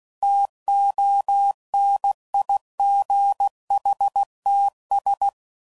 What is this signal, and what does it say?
Morse code, heard as a keyed tone, about 16 wpm, 800 Hz. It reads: TONIGHTS
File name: CW Code Groups @15WPM / Weather Sets / Weather 4 24